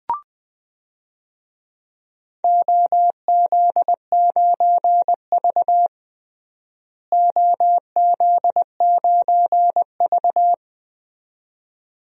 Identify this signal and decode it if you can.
Morse code, heard as a keyed tone, about 20 wpm, 700 Hz. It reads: OZ9V OZ9V
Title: DXCC-100 Call Signs - 4 Characters - Repeated 2x - 20wpm